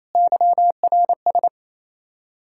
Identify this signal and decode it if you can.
Morse code, heard as a keyed tone, about 28 wpm, 700 Hz. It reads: YRS
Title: Practice QSO - Exchange 3 - More Info - 28wpm